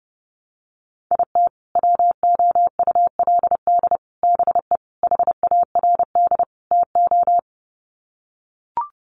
Morse code, heard as a keyed tone, about 30 wpm, 700 Hz. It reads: IT WOULD BE HARD TO
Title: Sets of 5 Words Encore2 30wpm